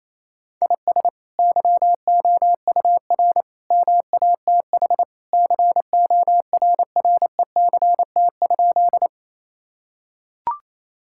Morse code, heard as a keyed tone, about 28 wpm, 700 Hz. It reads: IS YOUR MATH CORRECT?
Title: Sentences from Top 1000 Words - Review Part 6 - 28wpm